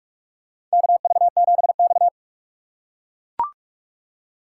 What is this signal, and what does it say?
Morse code, heard as a keyed tone, about 43 wpm, 700 Hz. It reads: KV7X
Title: USA Call Signs - 4 Characters - 3x Character Spacing - 45wpm